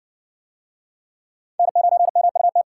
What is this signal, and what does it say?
Morse code, heard as a keyed tone, about 54 wpm, 700 Hz. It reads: N9MUT